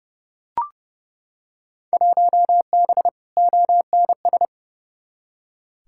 Morse code, heard as a keyed tone, about 31 wpm, 700 Hz. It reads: 1B ONS
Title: Field Day Exchanges Encore - 30wpm